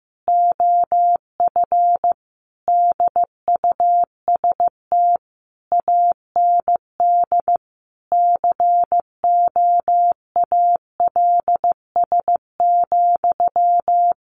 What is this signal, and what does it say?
Morse code, heard as a keyed tone, about 15 wpm, 700 Hz. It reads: OF DUST AND COALS,
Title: A Sound Of Thunder 15wpm